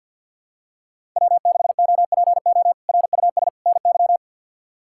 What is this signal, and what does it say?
Morse code, heard as a keyed tone, about 53 wpm, 700 Hz. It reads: W6OPO RUS NY
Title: K1USN SST QSO Practice - 50wpm